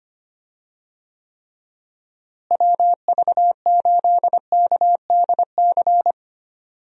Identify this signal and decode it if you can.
Morse code, heard as a keyed tone, about 25 wpm, 700 Hz. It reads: WV8KDC
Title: USA Call Signs - 6 Characters - Repeated 2x - 25wpm